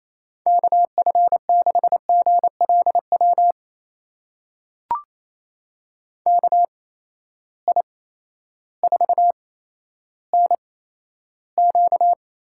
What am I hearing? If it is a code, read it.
Morse code, heard as a keyed tone, about 28 wpm, spaced as for 13 wpm, 700 Hz. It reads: KF6GLW KI4NQ
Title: USA Call Signs - 6 Characters - 8x Character Spacing - 28wpm